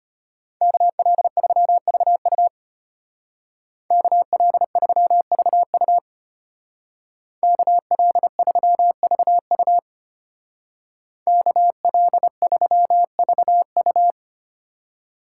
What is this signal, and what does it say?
Morse code, heard as a keyed tone, about 30 wpm, 700 Hz. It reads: KL3VU KL3VU KL3VU KL3VU
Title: Speed Racing - USA 2x2 Call Signs 25wpm